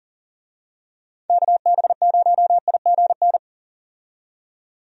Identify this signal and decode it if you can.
Morse code, heard as a keyed tone, about 40 wpm, 700 Hz. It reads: KB0IGN
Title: USA Call Signs - 6 Characters - 40wpm